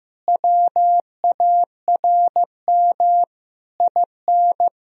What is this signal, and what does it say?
Morse code, heard as a keyed tone, about 15 wpm, 700 Hz. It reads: WARM IN
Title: Sentences from Top 1000 Words - Review Part 4 - 15wpm